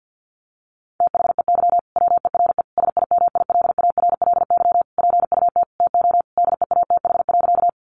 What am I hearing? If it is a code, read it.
Morse code, heard as a keyed tone, about 50 wpm, 700 Hz. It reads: THEY WERE SIMILARLY PUT TO DEATH.
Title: Sentences from Top 1400 Words 50wpm